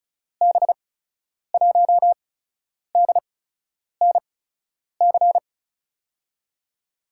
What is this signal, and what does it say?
Morse code, heard as a keyed tone, about 36 wpm, spaced as for 10 wpm, 700 Hz. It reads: B1DNC
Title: USA Call Signs - 6 Characters - 8x Character Spacing - 35wpm